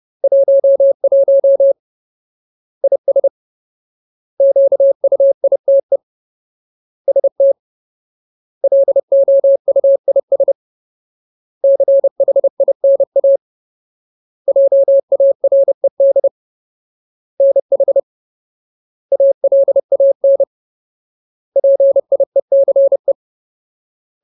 Morse code, heard as a keyed tone, about 30 wpm, 550 Hz. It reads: 11 IS QUITE ST LOUIS CHINA JARED NH ALAN PIECE
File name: Ultimate Word List / 30WPM Ultimate